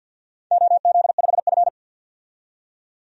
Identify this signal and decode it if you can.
Morse code, heard as a keyed tone, about 51 wpm, 700 Hz. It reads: OZ5F
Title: DXCC-100 Call Signs - 4 Characters - 50wpm